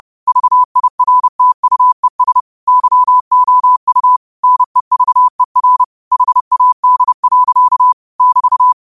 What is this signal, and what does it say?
Morse code, heard as a keyed tone, about 30 wpm, 1000 Hz. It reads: VIRTUES YOU NEVER HAD. <BT>